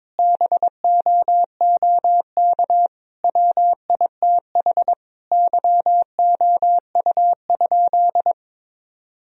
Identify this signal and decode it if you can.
Morse code, heard as a keyed tone, about 22 wpm, 700 Hz. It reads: BOOK WITH YOU?
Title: Sentences from Top 1000 Words - Review Part 3 - 22wpm